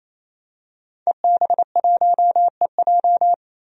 Morse code, heard as a keyed tone, about 28 wpm, 700 Hz. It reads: EB1EJ
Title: DXCC-100 Call Signs - 5 Characters - 28wpm